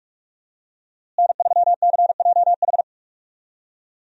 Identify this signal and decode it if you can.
Morse code, heard as a keyed tone, about 46 wpm, 700 Hz. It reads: N3CJH
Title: USA Call Signs - 5 Characters - Repeated 2x - 45wpm